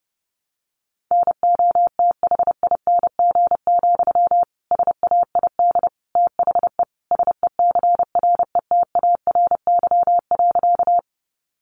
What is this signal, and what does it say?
Morse code, heard as a keyed tone, about 30 wpm, 700 Hz. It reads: NOTHING, SAID THE SECRETARY.
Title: Sentences from Top 1300 Words 30wpm